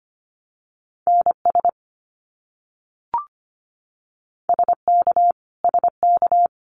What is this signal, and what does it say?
Morse code, heard as a keyed tone, about 25 wpm, 700 Hz. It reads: NS SK SK
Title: Canadian Provinces and Territories - Repeated 3x - 25wpm